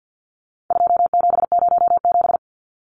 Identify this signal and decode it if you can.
Morse code, heard as a keyed tone, about 50 wpm, 700 Hz. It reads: 2707
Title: Numbers - 4-Digit 50wpm